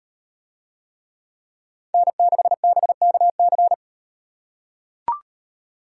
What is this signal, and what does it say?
Morse code, heard as a keyed tone, about 38 wpm, 700 Hz. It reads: N6BKC